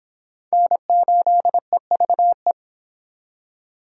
Morse code, heard as a keyed tone, about 26 wpm, 700 Hz. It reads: N8EVE